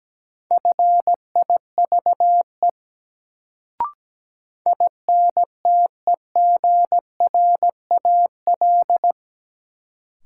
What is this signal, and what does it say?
Morse code, heard as a keyed tone, about 17 wpm, 700 Hz. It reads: FIVE INTEGRAL
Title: Sets of 5 Words - New Word Drill - 17wpm